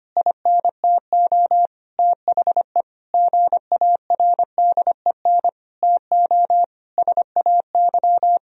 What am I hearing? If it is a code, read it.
Morse code, heard as a keyed tone, about 25 wpm, 700 Hz. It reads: INTO THE GARDEN TO SAY
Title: Sentences from Top 1000 Words - Review Part 4 - 25wpm